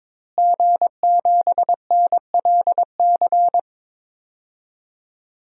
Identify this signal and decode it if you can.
Morse code, heard as a keyed tone, about 22 wpm, 700 Hz. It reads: G7NLC
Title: DXCC-100 Call Signs - 5 Characters - Repeated 2x - 22wpm